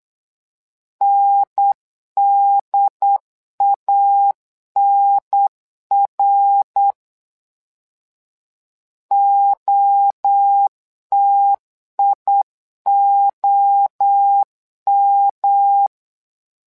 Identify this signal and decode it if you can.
Morse code, heard as a keyed tone, about 8 wpm, 800 Hz. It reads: NDANR OTIOM
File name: LEZIONI / 3 MWRO / 3 MWRO 8wpm 2606